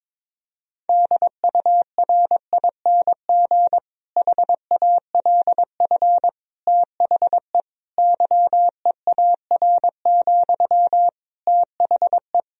Morse code, heard as a keyed tone, about 21 wpm, 700 Hz. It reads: DURING HALF THE YEAR, THE
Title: Sentences from Top 1000 Words - Review Part 2 - 22wpm